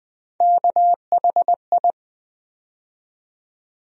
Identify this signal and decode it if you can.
Morse code, heard as a keyed tone, about 20 wpm, 700 Hz. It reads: KHI
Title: USA Call Sign Postfixes - 3 Letters 20wpm